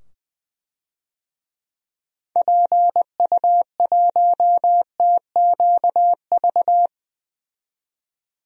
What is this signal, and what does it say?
Morse code, heard as a keyed tone, about 20 wpm, 700 Hz. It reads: PU1TQV
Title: DXCC-100 Call Signs - 6 Characters - 20wpm